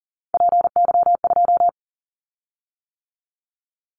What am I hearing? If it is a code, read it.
Morse code, heard as a keyed tone, about 40 wpm, 700 Hz. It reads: PY2
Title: Callsign Prefixes for Top 50 DXCC - 40wpm